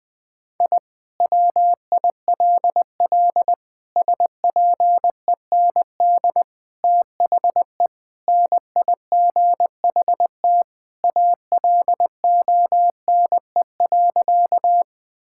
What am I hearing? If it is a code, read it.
Morse code, heard as a keyed tone, about 19 wpm, 700 Hz. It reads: I WILL SPEND THE NIGHT ALONE.
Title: Sentences from Top 1000 Words - Review Part 5 - 20wpm